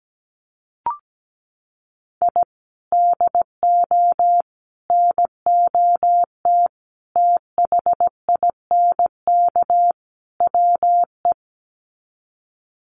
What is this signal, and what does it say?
Morse code, heard as a keyed tone, about 17 wpm, 700 Hz. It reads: I DO NOT THINK WE
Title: Sets of 5 Words 17wpm